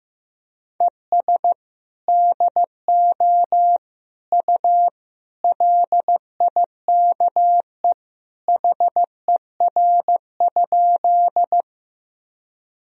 Morse code, heard as a keyed tone, about 15 wpm, 700 Hz. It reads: ES DO U LIKE HER?
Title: Sentences from Top 100 Words and Abbreviations 15wpm